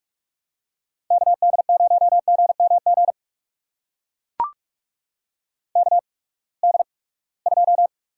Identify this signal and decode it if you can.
Morse code, heard as a keyed tone, about 42 wpm, spaced as for 25 wpm, 700 Hz. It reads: KD0GMG KD2
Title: USA Call Signs - 6 Characters - 8x Character Spacing - 45wpm